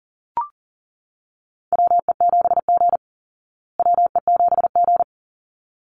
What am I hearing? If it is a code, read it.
Morse code, heard as a keyed tone, about 40 wpm, 700 Hz. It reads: WE7G WE7G
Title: Call Signs Repeated 3x - 40wpm